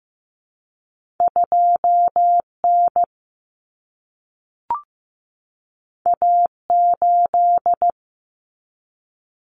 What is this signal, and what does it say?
Morse code, heard as a keyed tone, about 15 wpm, 700 Hz. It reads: 2N A8
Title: Common Cut Numbers - 2-Digit 15wpm